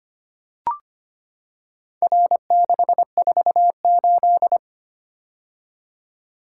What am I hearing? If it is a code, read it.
Morse code, heard as a keyed tone, about 25 wpm, 700 Hz. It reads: R648
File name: practice / 25wpm / Four Letter-Number Combinations - 25wpm